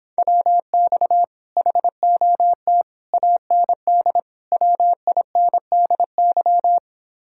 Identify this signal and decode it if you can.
Morse code, heard as a keyed tone, about 29 wpm, 700 Hz. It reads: WX HOT AND WINDY